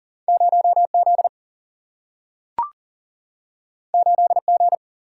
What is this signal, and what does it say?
Morse code, heard as a keyed tone, about 40 wpm, 700 Hz. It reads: 0Z 8G
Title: Single-Letter Single-Number 40wpm